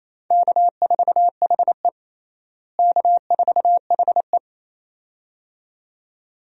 Morse code, heard as a keyed tone, about 28 wpm, 700 Hz. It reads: K4HE K4HE
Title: USA Call Signs - 4 Characters - Repeated 2x - 28wpm